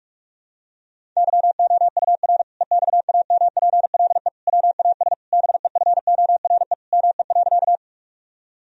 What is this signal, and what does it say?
Morse code, heard as a keyed tone, about 46 wpm, 700 Hz. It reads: YOUR EXAMPLE WAS BEFORE ME.
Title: Sentences from Top 1000 Words - Review Part 1 - 45wpm